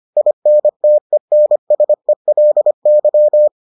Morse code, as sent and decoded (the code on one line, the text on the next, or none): .. -. - . -. ... .-.. -.--
INTENSELY